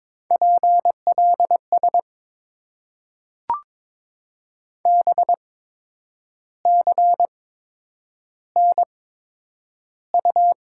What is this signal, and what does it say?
Morse code, heard as a keyed tone, about 22 wpm, spaced as for 8 wpm, 690 Hz. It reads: PLS BCNU
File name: Top 90 QSO Elements - 8x Character Spacing - 22wpm